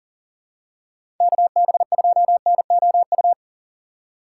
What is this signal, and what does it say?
Morse code, heard as a keyed tone, about 40 wpm, 700 Hz. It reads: KB2NOU
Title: USA Call Signs - 6 Characters - 3x Character Spacing - 40wpm